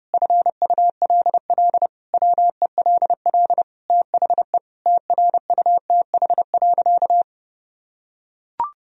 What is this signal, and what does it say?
Morse code, heard as a keyed tone, about 30 wpm, 700 Hz. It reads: FULL WELL THE TRUTH.
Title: Sentences from Top 1000 Words - Review Part 3 - 30wpm